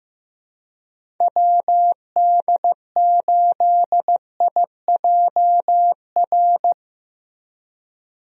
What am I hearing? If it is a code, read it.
Morse code, heard as a keyed tone, about 15 wpm, 700 Hz. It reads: WD8IJR
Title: USA Call Signs - 6 Characters - Repeated 3x - 15wpm